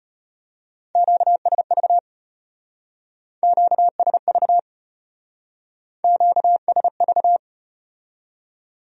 Morse code, heard as a keyed tone, about 32 wpm, 700 Hz. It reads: QSV QSV QSV